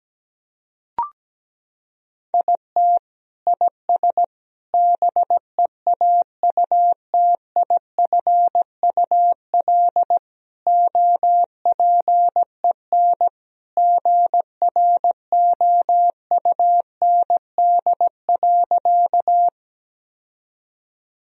Morse code, heard as a keyed tone, about 17 wpm, 700 Hz. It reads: IT IS BEAUTIFUL OPEN GROUND.
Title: Sentences from Top 1000 Words - Review Part 2 - 17wpm